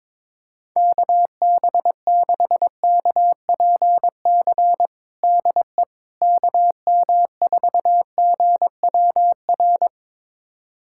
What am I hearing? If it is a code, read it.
Morse code, heard as a keyed tone, about 22 wpm, 700 Hz. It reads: KB6KPC DE KM4GWR
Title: USA Callsign DE Callsign - 6 Characters - 22wpm